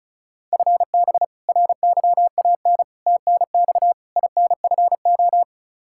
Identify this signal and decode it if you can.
Morse code, heard as a keyed tone, about 34 wpm, 700 Hz. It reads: FB RYAN TNX INFO
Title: K1USN SST QSO Practice 35wpm